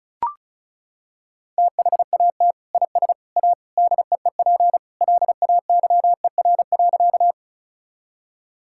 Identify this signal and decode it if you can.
Morse code, heard as a keyed tone, about 34 wpm, 700 Hz. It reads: THAT IS A DEEP LAYER.
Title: Sentences from Top 1600 Words 35wpm